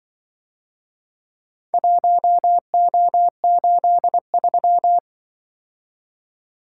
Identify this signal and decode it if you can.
Morse code, heard as a keyed tone, about 23 wpm, 700 Hz. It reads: 1O83
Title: Speed-Racing - Four Letter-Number Combinations - 20wpm